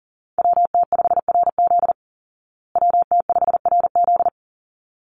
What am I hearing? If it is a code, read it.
Morse code, heard as a keyed tone, about 40 wpm, 700 Hz. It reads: WT5RZ WT5RZ
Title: Call Signs Repeated 3x - 40wpm